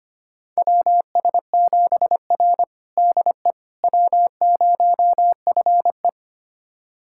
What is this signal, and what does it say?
Morse code, heard as a keyed tone, about 25 wpm, 700 Hz. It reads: WS7R DE W0FE
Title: USA Callsign DE Callsign - 4 Characters - 25wpm